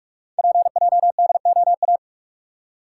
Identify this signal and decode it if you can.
Morse code, heard as a keyed tone, about 44 wpm, 690 Hz. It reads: PJDOA
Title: Code Groups - 5 Letters - 45wpm